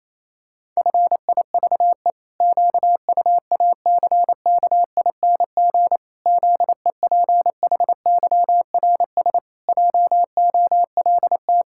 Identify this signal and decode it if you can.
Morse code, heard as a keyed tone, about 27 wpm, 700 Hz. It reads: FIVE QUACKING ZEPHYRS JOLT
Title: Pangram Challenge v2 - 28wpm